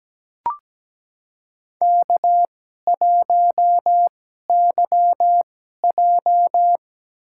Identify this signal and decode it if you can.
Morse code, heard as a keyed tone, about 17 wpm, spaced as for 12 wpm, 700 Hz. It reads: K1YJ
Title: USA Call Signs - 2x Character Spacing - 17wpm